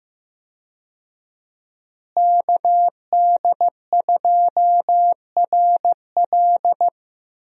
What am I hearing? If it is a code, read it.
Morse code, heard as a keyed tone, about 15 wpm, 700 Hz. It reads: KD2RL
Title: USA Call Signs - 5 Characters - Repeated 2x - 15wpm